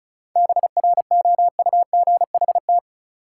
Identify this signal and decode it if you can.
Morse code, heard as a keyed tone, about 36 wpm, 700 Hz. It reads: BROUGHT
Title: Sentences from Top 500 Words - New Word Drill - 35wpm